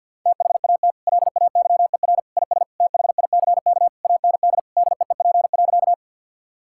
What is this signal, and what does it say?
Morse code, heard as a keyed tone, about 50 wpm, 700 Hz. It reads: THAT LAYER IS THICK AND DEEP.
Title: Sentences from Top 1600 Words 50wpm